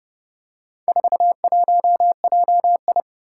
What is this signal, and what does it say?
Morse code, heard as a keyed tone, about 30 wpm, 700 Hz. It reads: 41JI
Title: Four Challenging Letter-Number Combinations 30wpm